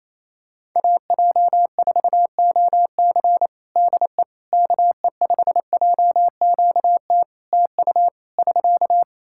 Morse code, heard as a keyed tone, about 28 wpm, 700 Hz. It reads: AJ4OC DE KE5JQT TU <SK>
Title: K1USN SST QSO Practice 28wpm